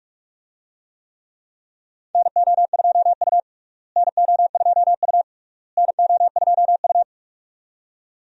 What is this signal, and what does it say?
Morse code, heard as a keyed tone, about 46 wpm, 700 Hz. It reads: NO2U NO2U NO2U
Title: USA Call Signs - 4 Characters - Repeated 3x - 45wpm